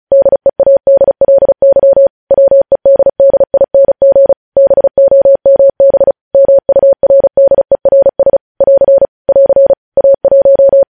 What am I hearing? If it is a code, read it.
Morse code, heard as a keyed tone, about 36 wpm, 550 Hz. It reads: DEADLY WEDDING BOMB MURDERS <AR> <AR> A1
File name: podcast / News headlines-1x-35x35